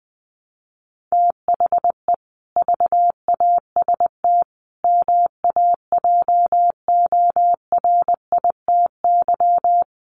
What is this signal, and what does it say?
Morse code, heard as a keyed tone, about 20 wpm, 700 Hz. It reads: THE VAST MAJORITY
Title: Sets of 4 Words 20wpm